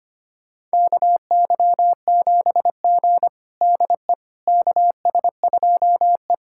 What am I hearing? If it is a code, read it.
Morse code, heard as a keyed tone, about 25 wpm, 700 Hz. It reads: KY7G DE KS2E
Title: USA Callsign DE Callsign - 4 Characters - 25wpm